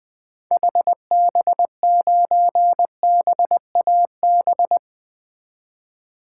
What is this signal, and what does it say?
Morse code, heard as a keyed tone, about 20 wpm, 700 Hz. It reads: HB9BAB
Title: 2023 SOTA Activator Call Signs - 20wpm